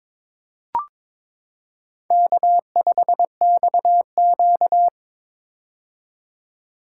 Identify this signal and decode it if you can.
Morse code, heard as a keyed tone, about 22 wpm, 700 Hz. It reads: K5XQ
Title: K1USN SST QSO Practice - 22wpm